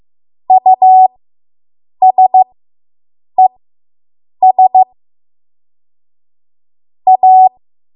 Morse code, heard as a keyed tone, about 15 wpm, spaced as for 7 wpm, 750 Hz. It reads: USES A